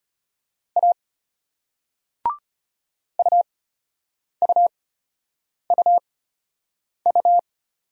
Morse code, heard as a keyed tone, about 34 wpm, spaced as for 8 wpm, 700 Hz. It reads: A UUUU